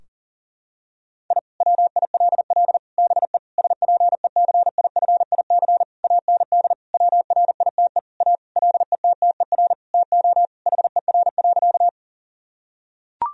Morse code, heard as a keyed tone, about 40 wpm, 700 Hz. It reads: I WILL BE SPECIFIC AND WRITE A LETTER TO HER.
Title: Sentences from Top 1000 Words - Review Part 3 - 40wpm